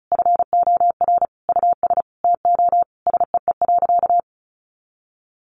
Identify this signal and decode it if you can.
Morse code, heard as a keyed tone, about 34 wpm, 700 Hz. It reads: FOR US TO SEE.
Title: Sentences from Top 500 Words 35wpm